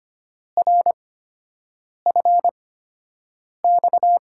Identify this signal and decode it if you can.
Morse code, heard as a keyed tone, about 25 wpm, spaced as for 7 wpm, 700 Hz. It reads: RFX
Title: USA Call Signs - 8x Character Spacing - 25wpm